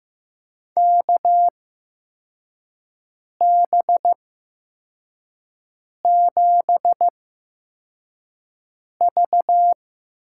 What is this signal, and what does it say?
Morse code, heard as a keyed tone, about 15 wpm, spaced as for 4 wpm, 700 Hz. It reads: KB7V
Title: USA Call Signs - 6 Characters - 8x Character Spacing - 15wpm